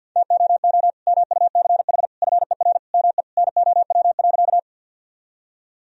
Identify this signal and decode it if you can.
Morse code, heard as a keyed tone, about 51 wpm, 700 Hz. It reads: TOO MUCH FER ME NOW.
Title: Sentences from Top 200 Words and Abbreviations 50wpm